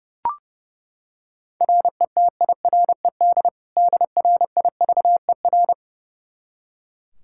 Morse code, heard as a keyed tone, about 30 wpm, 700 Hz. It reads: RETIRED DRIVER
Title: K1USN SST QSO Practice 30wpm